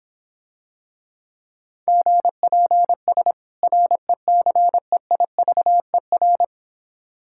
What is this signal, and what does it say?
Morse code, heard as a keyed tone, about 26 wpm, 700 Hz. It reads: GPS RECEIVER